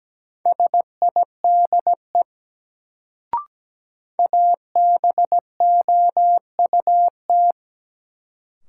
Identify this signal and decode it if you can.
Morse code, heard as a keyed tone, about 17 wpm, 700 Hz. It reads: SIDE ABOUT
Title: Sets of 2 Words - New Word Drill - 17wpm